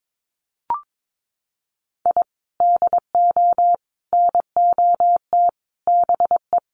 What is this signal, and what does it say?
Morse code, heard as a keyed tone, about 22 wpm, 700 Hz. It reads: I DO NOT BE
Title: Sets of 4 Words 22wpm